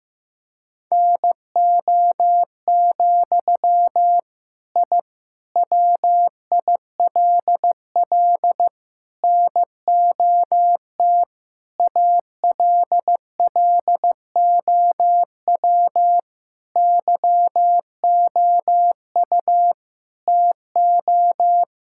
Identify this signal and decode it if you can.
Morse code, heard as a keyed tone, about 16 wpm, 700 Hz. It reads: NO, I WILL NOT ALLOW YOU TO
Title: Sentences from Top 1000 Words - Review Part 5 - 15wpm